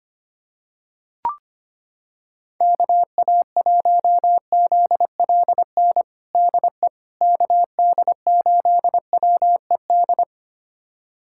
Morse code, heard as a keyed tone, about 25 wpm, 700 Hz. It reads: KA1ZLN DE KD8WED
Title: USA Callsign DE Callsign - 6 Characters - 25wpm